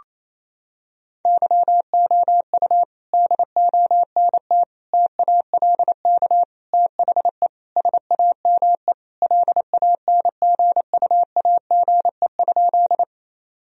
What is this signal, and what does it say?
Morse code, heard as a keyed tone, about 27 wpm, 700 Hz. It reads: YOU DONT TALK THE SAME LANGUAGE?
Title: Sentences from Top 1000 Words - Review Part 3 - 28wpm